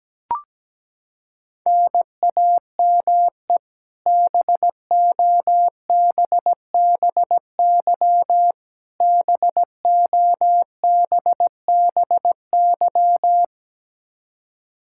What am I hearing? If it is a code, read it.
Morse code, heard as a keyed tone, about 17 wpm, 700 Hz. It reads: NAME BOBBY BOBBY
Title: Practice QSO - Exchange 1 - Beginning - 17wpm